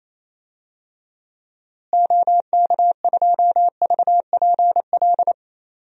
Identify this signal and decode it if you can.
Morse code, heard as a keyed tone, about 28 wpm, 700 Hz. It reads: OK2VPL
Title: DXCC-100 Call Signs - 6 Characters - Repeated 2x - 28wpm